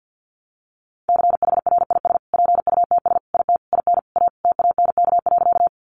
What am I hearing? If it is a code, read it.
Morse code, heard as a keyed tone, about 50 wpm, 700 Hz. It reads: CHRIS PUTS IT IN A TANK.
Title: Easy Phrases 50wpm